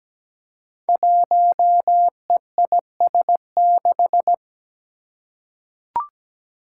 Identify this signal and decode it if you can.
Morse code, heard as a keyed tone, about 17 wpm, 700 Hz. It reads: 1EIS6
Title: Five Challenging Letter-Number Combinations 17wpm